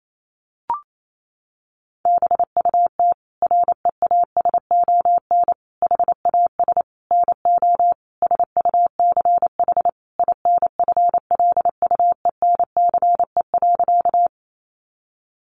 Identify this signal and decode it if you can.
Morse code, heard as a keyed tone, about 28 wpm, 700 Hz. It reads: BUT REASON HAS NO SUCH INFLUENCE.
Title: Sentences from Top 1100 Words 28wpm